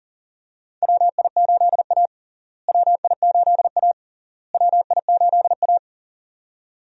Morse code, heard as a keyed tone, about 40 wpm, 700 Hz. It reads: WI8A WI8A WI8A
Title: USA Call Signs - 4 Characters - Repeated 3x - 40wpm